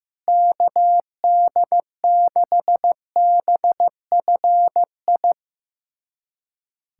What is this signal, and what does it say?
Morse code, heard as a keyed tone, about 15 wpm, 700 Hz. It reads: KD6BFI
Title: USA Callsign DE Callsign 15wpm